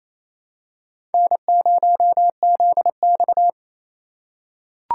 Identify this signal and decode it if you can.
Morse code, heard as a keyed tone, about 28 wpm, 700 Hz. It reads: N0ZX
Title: USA Call Signs - 4 Characters - 4x Character Spacing - 28wpm